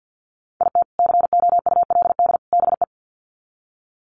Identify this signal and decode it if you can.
Morse code, heard as a keyed tone, about 50 wpm, 700 Hz. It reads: IT COULD BE
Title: Sets of 3 Words 50wpm